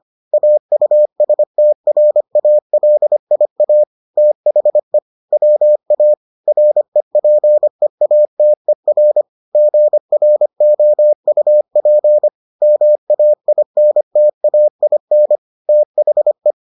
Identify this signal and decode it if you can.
Morse code, heard as a keyed tone, about 25 wpm, 600 Hz. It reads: AUSTRALIA THE WA REPEATER GROUP MAINTAIN THE